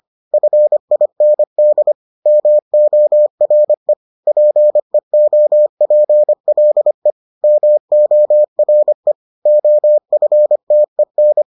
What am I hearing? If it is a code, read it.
Morse code, heard as a keyed tone, about 25 wpm, 600 Hz. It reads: FIND MORE PEOPLE MORE OFTEN